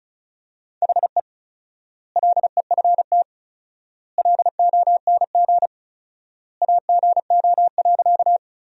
Morse code, heard as a keyed tone, about 34 wpm, 700 Hz. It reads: HE LEFT LONG AGO.
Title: 4x Word-Spacing - Sentences from Top 500 Words - 35wpm